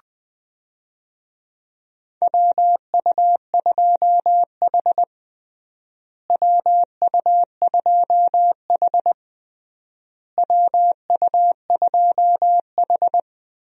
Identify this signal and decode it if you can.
Morse code, heard as a keyed tone, about 20 wpm, 700 Hz. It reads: WU2H WU2H WU2H
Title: USA Call Signs - 4 Characters - Repeated 3x - 20wpm